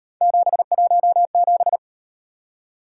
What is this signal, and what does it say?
Morse code, heard as a keyed tone, about 38 wpm, 700 Hz. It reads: 717